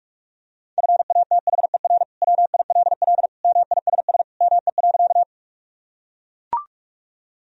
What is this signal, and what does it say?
Morse code, heard as a keyed tone, about 41 wpm, 700 Hz. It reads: FATHER WILL MISS ME.
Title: Sentences from Top 1000 Words - Review Part 3 - 45wpm